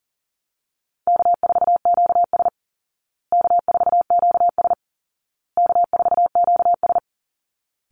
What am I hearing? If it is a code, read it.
Morse code, heard as a keyed tone, about 37 wpm, 700 Hz. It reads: K4QS K4QS K4QS
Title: Call Signs Repeated 2x - 40wpm